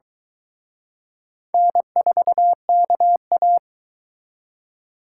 Morse code, heard as a keyed tone, about 22 wpm, 700 Hz. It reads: N4KA